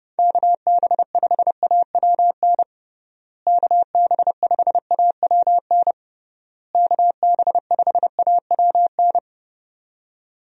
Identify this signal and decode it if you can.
Morse code, heard as a keyed tone, about 30 wpm, 700 Hz. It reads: KB5AWN KB5AWN KB5AWN
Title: USA Call Signs - 6 Characters - Repeated 3x - 30wpm